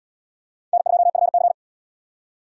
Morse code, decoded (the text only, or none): N2FC